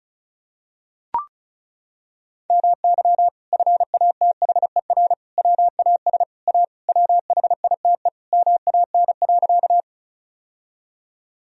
Morse code, heard as a keyed tone, about 34 wpm, 700 Hz. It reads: MY FATHER WAS A WHITE MAN.
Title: Sentences from Top 1000 Words - Review Part 1 - 35wpm